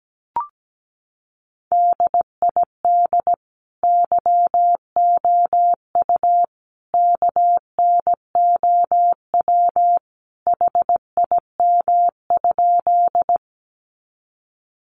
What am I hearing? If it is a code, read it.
Morse code, heard as a keyed tone, about 17 wpm, 700 Hz. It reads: DID YOU KNOW HIM?
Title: Sentences from Top 100 Words 17wpm